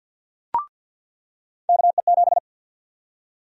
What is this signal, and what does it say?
Morse code, heard as a keyed tone, about 46 wpm, 700 Hz. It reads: KE7
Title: USA Call Sign Prefixes - 50wpm